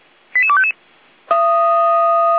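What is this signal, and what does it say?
Five-tone selective call (ZVEI), heard as quick runs of five tones, tones of about 70 ms. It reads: 88255
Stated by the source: ZVEI five-tone call 88191 received by SDR.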